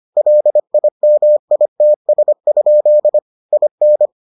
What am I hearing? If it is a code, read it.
Morse code, heard as a keyed tone, about 25 wpm, 600 Hz. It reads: LIMITS? IN